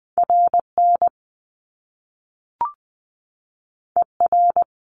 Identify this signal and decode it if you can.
Morse code, heard as a keyed tone, about 19 wpm, 700 Hz. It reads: RN ER
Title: Top Two Letter Combinations in Words 20wpm